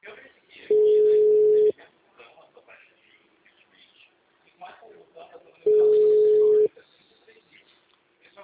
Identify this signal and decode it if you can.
Ringback tone (Europe, 425 Hz), with speech.
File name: out- -2015- - - .93828